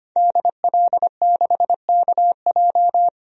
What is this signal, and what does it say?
Morse code, heard as a keyed tone, about 25 wpm, 700 Hz. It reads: DL6KJ
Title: DXCC-100 Call Signs - 5 Characters - 25wpm